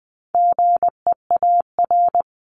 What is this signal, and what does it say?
Morse code, heard as a keyed tone, about 20 wpm, 700 Hz. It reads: GEAR
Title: Top 1251-1500 Words 20wpm